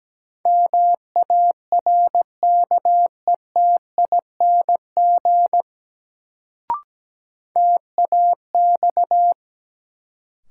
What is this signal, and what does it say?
Morse code, heard as a keyed tone, about 17 wpm, 700 Hz. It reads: MARKETING TAX
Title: Sentences from Top 1000 Words - Review Part 5 - New Word Drill - 17wpm